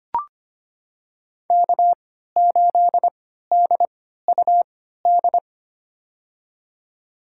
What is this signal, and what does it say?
Morse code, heard as a keyed tone, about 25 wpm, spaced as for 14 wpm, 700 Hz. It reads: K8DUD
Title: USA Call Signs - 3x Character Spacing - 25wpm